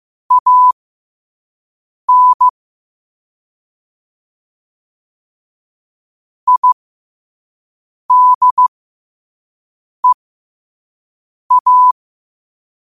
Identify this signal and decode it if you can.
Morse code, heard as a keyed tone, about 15 wpm, spaced as for 5 wpm, 1000 Hz. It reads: AN IDEA
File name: qotdfiles / 5wpm